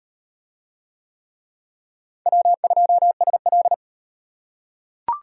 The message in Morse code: W2SL

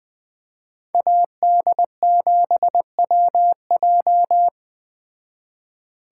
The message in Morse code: AD7WJ

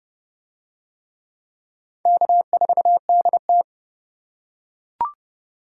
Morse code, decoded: K4DT

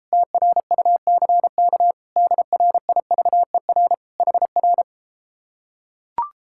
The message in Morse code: TRUCK DRIVER HR

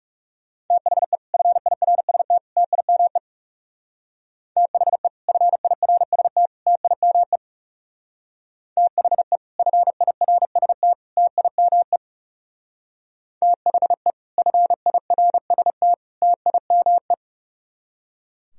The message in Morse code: THE FIRST TIME THE FIRST TIME THE FIRST TIME THE FIRST TIME

